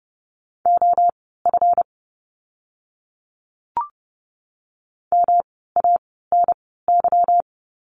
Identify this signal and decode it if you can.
Morse code, heard as a keyed tone, about 30 wpm, spaced as for 17 wpm, 700 Hz. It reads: OF MANY
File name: course / 30:10wpm / 237 Lesson 38f - Letter q - Words 30wpm